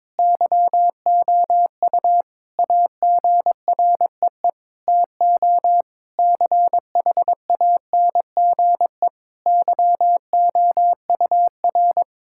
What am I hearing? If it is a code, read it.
Morse code, heard as a keyed tone, about 22 wpm, 700 Hz. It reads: YOU AGREE TO CHANGE YOUR
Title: Sentences from Top 1000 Words - Review Part 6 - 22wpm